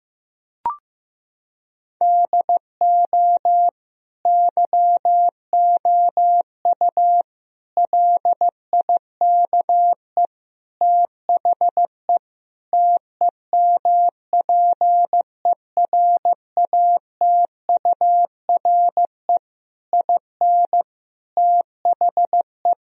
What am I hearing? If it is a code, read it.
Morse code, heard as a keyed tone, about 16 wpm, 700 Hz. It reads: DO YOU LIKE THE TEMPERATURE IN THE
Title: Sentences from Top 1000 Words - Review Part 4 - 15wpm